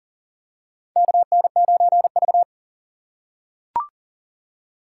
Morse code, heard as a keyed tone, about 40 wpm, 700 Hz. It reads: KN9V
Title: USA Call Signs - 4 Characters - 2x Character Spacing - 40wpm